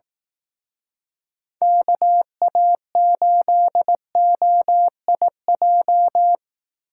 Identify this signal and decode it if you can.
Morse code, heard as a keyed tone, about 18 wpm, 700 Hz. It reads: KA8OIJ
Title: Speed Racing - USA 2x3 Call Signs 15wpm